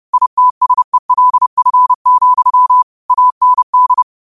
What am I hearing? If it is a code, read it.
Morse code, heard as a keyed tone, about 30 wpm, 1000 Hz. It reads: ITSELF, AND
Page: Quotes for Mon, 28 Apr 2025 in Morse Code at 30 words per minute.